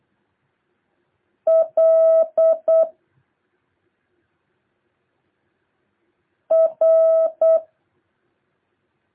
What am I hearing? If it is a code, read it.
Morse code, heard as a keyed tone, about 8 wpm, 650 Hz. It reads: L R
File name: jh8cbh / 2023cw